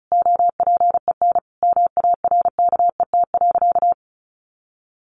Morse code, heard as a keyed tone, about 34 wpm, 700 Hz. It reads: OPEN MARKET.
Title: Sentences from Top 1400 Words 35wpm